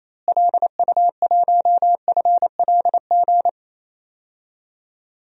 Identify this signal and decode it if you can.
Morse code, heard as a keyed tone, about 28 wpm, 700 Hz. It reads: LU1FLG
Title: DXCC-100 Call Signs - 6 Characters - Repeated 3x - 28wpm